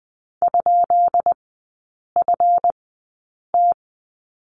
Morse code, heard as a keyed tone, about 20 wpm, spaced as for 8 wpm, 700 Hz. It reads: ?FT
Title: 138 Lesson 23e - Letter f - Three Characters 20wpm